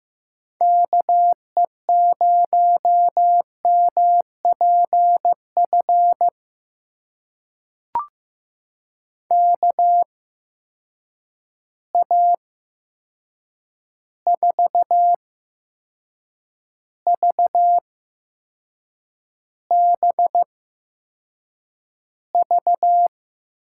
Morse code, heard as a keyed tone, about 15 wpm, spaced as for 6 wpm, 700 Hz. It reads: KE0MPF KA4VBV
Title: USA Call Signs - 6 Characters - 8x Character Spacing - 15wpm